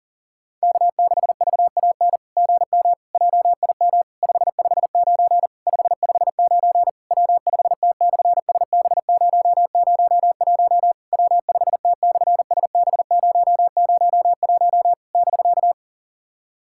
Morse code, heard as a keyed tone, about 37 wpm, 690 Hz. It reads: K6VAN GM JIM 559 559 W5T/SB001 W5T/SB001 <BK>
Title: SOTA QSO Practice 40wpm